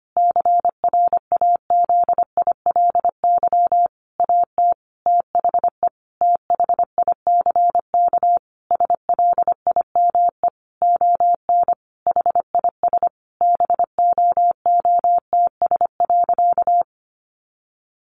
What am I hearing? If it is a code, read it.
Morse code, heard as a keyed tone, about 25 wpm, 700 Hz. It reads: CRAZILY AT THE THICK SLIME ON HIS BOOTS.
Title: A Sound Of Thunder 25wpm